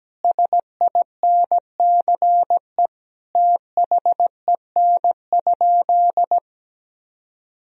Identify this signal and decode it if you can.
Morse code, heard as a keyed tone, about 17 wpm, 700 Hz. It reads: SINCE THEN?
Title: Sentences from Top 1000 Words - Review Part 1 - 17wpm